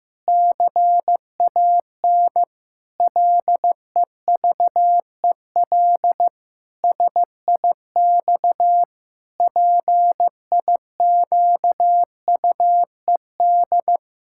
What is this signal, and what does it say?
Morse code, heard as a keyed tone, about 15 wpm, 700 Hz. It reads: CAN LEVEL SIX PIQUED
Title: Pangram Challenge v2 - 15wpm